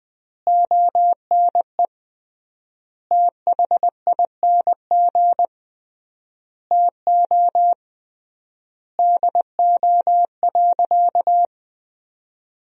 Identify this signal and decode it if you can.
Morse code, heard as a keyed tone, about 20 wpm, 700 Hz. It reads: ONE THING TO DO.
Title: 3x Word-Spacing - Sentences from Top 100 Words - 20wpm